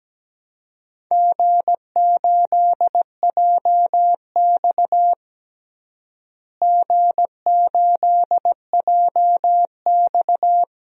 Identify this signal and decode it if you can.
Morse code, heard as a keyed tone, about 17 wpm, 700 Hz. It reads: G8JX G8JX
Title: DXCC-100 Call Signs - 4 Characters - Repeated 2x - 17wpm